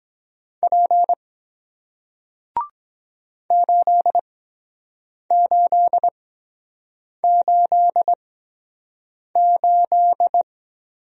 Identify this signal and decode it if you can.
Morse code, heard as a keyed tone, about 24 wpm, spaced as for 7 wpm, 700 Hz. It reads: P 8888